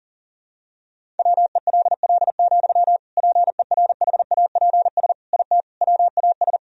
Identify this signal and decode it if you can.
Morse code, heard as a keyed tone, about 40 wpm, 700 Hz. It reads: WELL, PERHAPS IT WAS
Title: Sentences from Top 500 Words and Abbreviations 40wpm